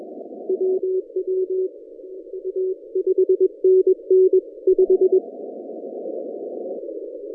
Morse code, heard as a keyed tone, about 21 wpm, spaced as for 16 wpm, 380 Hz. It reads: WWA 5NN 5
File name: 2015 cq ww cw / 2015 cqwww k5tr - 0-2